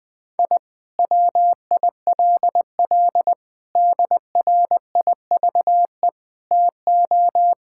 Morse code, heard as a keyed tone, about 20 wpm, 700 Hz. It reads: I WILL DRIVE TO
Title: Sentences from Top 1000 Words - Review Part 6 - 20wpm